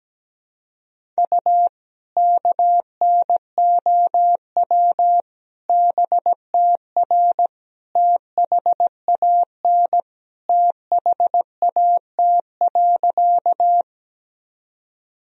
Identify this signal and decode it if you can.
Morse code, heard as a keyed tone, about 17 wpm, 700 Hz. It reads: U KNOW BTR THAN THAT.
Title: Sentences from Top 400 Words and Abbreviations 17wpm